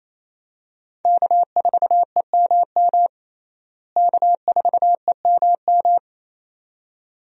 Morse code, heard as a keyed tone, about 27 wpm, 700 Hz. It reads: K4EMM K4EMM
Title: USA Call Signs - 5 Characters - Repeated 2x - 28wpm